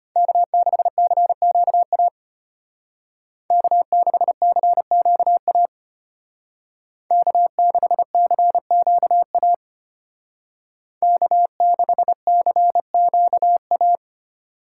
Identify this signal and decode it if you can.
Morse code, heard as a keyed tone, about 30 wpm, 700 Hz. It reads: K6CQA K6CQA K6CQA K6CQA
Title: Speed Racing - USA 1x3 Call Signs 25wpm